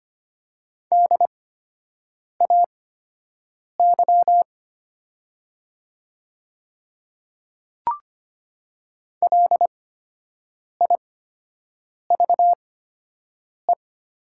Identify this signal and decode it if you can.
Morse code, heard as a keyed tone, about 25 wpm, spaced as for 7 wpm, 700 Hz. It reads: DAY LIVE
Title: 8x Character Spacing - Top 500 Words - 25wpm